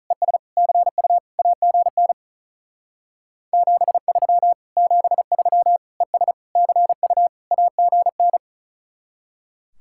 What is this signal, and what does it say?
Morse code, heard as a keyed tone, about 45 wpm, 700 Hz. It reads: ES CU AGN 73 73 ES CU AGN